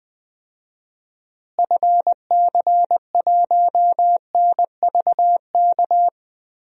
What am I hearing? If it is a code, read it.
Morse code, heard as a keyed tone, about 20 wpm, 700 Hz. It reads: FC1NVK
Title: DXCC-100 Call Signs - 6 Characters - Repeated 3x - 20wpm